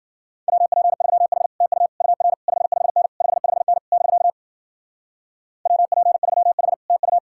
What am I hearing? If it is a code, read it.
Morse code, heard as a keyed tone, about 50 wpm, 700 Hz. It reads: WP3H TU UR 55N 55N <BK> WP3H TU
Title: Speed-Racing Common QSO Phrases Encore - 40wpm